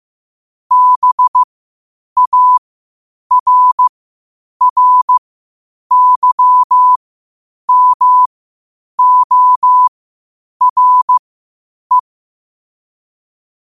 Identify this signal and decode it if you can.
Morse code, heard as a keyed tone, about 15 wpm, spaced as for 8 wpm, 1000 Hz. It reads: BARRYMORE